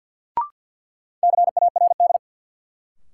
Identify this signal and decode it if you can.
Morse code, heard as a keyed tone, about 47 wpm, 700 Hz. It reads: CARD
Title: Sentences from Top 600 Words - New Word Drill - 50wpm